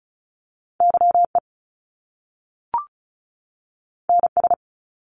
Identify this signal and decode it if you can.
Morse code, heard as a keyed tone, about 36 wpm, 700 Hz. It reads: YE NS
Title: Top Two Letter Combinations in Words 35wpm